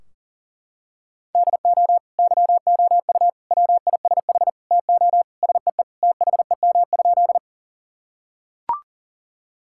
Morse code, heard as a keyed tone, about 40 wpm, 700 Hz. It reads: DO YOU WISH TO SEE THEM?
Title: Sentences from Top 1600 Words 40wpm